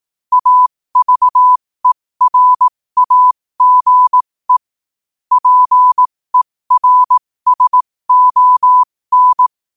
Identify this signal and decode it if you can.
Morse code, heard as a keyed tone, about 18 wpm, spaced as for 15 wpm, 1000 Hz. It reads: AVERAGE PERSON
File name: qotdfiles / 15wpm